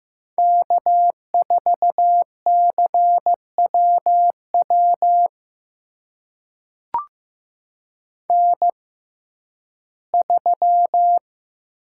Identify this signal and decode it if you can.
Morse code, heard as a keyed tone, about 15 wpm, spaced as for 11 wpm, 700 Hz. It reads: K4CWW N3
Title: USA Call Signs - 5 Characters - 6x Character Spacing - 15wpm